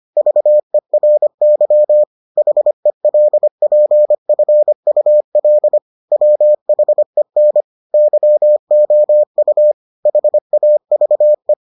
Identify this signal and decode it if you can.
Morse code, heard as a keyed tone, about 25 wpm, 600 Hz. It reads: VERY HELPFUL WHEN YOU HAVE